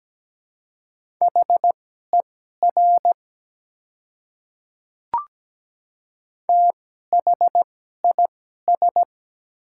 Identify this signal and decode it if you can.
Morse code, heard as a keyed tone, about 17 wpm, spaced as for 12 wpm, 700 Hz. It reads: HER THIS
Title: 2x Character Spacing - Top 100 Words - 17wpm